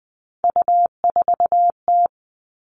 Morse code, heard as a keyed tone, about 20 wpm, 700 Hz. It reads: U4T
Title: All Cut Numbers - 3-Digit 20wpm